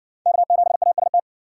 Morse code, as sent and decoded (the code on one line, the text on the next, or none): -.- --... .- ... -
K7AST